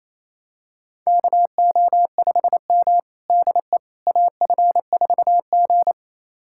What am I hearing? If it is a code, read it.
Morse code, heard as a keyed tone, about 28 wpm, 700 Hz. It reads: KO5M DE AF4G
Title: USA Callsign DE Callsign - 4 Characters - 28wpm